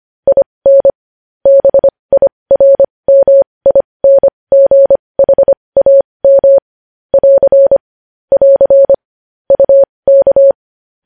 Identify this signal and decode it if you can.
Morse code, heard as a keyed tone, about 25 wpm, spaced as for 20 wpm, 550 Hz. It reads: IN BIRMINGHAM <AR> <AR> UK
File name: podcast / News headlines-1x-25x20